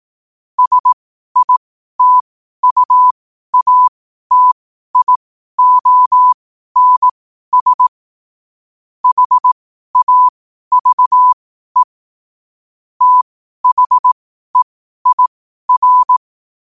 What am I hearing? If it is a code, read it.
Morse code, heard as a keyed tone, about 18 wpm, spaced as for 12 wpm, 1000 Hz. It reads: SITUATIONS HAVE THEIR
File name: qotdfiles / 12wpm